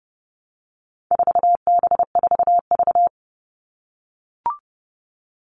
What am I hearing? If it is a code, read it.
Morse code, heard as a keyed tone, about 30 wpm, 700 Hz. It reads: 4B4V
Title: All Cut Numbers - 4-Digit 30wpm